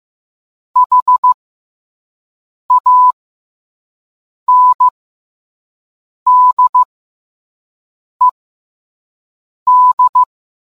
Morse code, heard as a keyed tone, about 15 wpm, spaced as for 5 wpm, 1000 Hz. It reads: HANDED